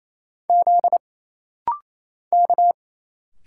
Morse code, heard as a keyed tone, about 28 wpm, 700 Hz. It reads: Z K
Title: Single Letters - Rapid-Fire 28wpm